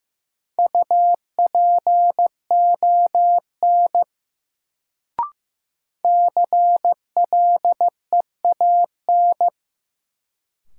Morse code, heard as a keyed tone, about 15 wpm, 700 Hz. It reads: UPON CLEAN
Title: Sentences from Top 1100 Words - New Word Drill Alt - 15wpm